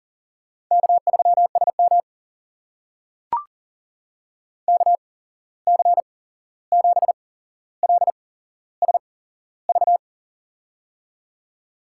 Morse code, heard as a keyed tone, about 39 wpm, spaced as for 15 wpm, 700 Hz. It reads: K3SM KC7LSV
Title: USA Call Signs - 8x Character Spacing - 40wpm